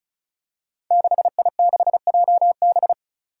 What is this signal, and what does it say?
Morse code, heard as a keyed tone, about 36 wpm, 700 Hz. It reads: 6I6JB
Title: Five Challenging Letter-Number Combinations 35wpm